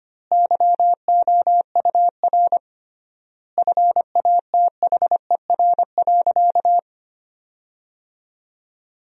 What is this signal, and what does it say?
Morse code, heard as a keyed tone, about 25 wpm, 700 Hz. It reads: YOUR FATHER.
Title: 3x Word-Spacing - Sentences from Top 300 Words - 25wpm